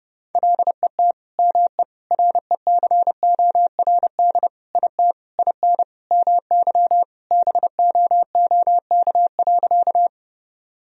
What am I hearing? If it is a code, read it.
Morse code, heard as a keyed tone, about 30 wpm, 700 Hz. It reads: LET ME RECORD IT IN MY BOOK.
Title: Sentences from Top 1000 Words - Review Part 3 - 30wpm